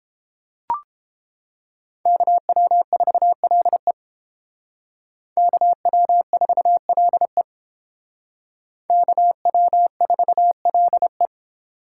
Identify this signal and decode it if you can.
Morse code, heard as a keyed tone, about 31 wpm, 700 Hz. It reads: KW4LE KW4LE KW4LE